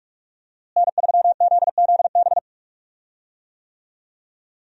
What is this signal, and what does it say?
Morse code, heard as a keyed tone, about 44 wpm, 700 Hz. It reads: N3ZZB
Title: USA Call Signs - 5 Characters - Repeated 3x - 45wpm